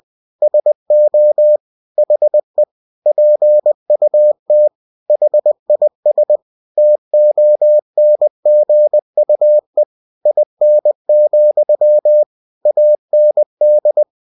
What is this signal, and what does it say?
Morse code, heard as a keyed tone, about 20 wpm, 600 Hz. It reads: SO HE PUT HIS TONGUE IN, AND
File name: cw-books / winnie-the-pooh / winnie-20wpm-page-0037